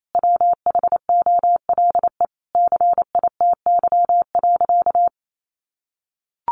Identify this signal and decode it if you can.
Morse code, heard as a keyed tone, about 28 wpm, 700 Hz. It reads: WHOLE CITY.
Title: Sentences from Top 1400 Words 28wpm